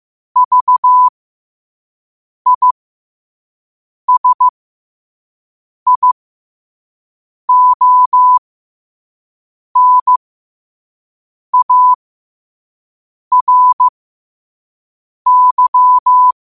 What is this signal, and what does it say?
Morse code, heard as a keyed tone, about 15 wpm, spaced as for 5 wpm, 1000 Hz. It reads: VISIONARY
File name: qotdfiles / 5wpm